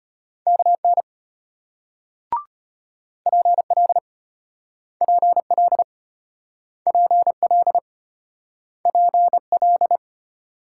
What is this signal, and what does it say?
Morse code, heard as a keyed tone, about 33 wpm, 700 Hz. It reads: KN PL PL PL PL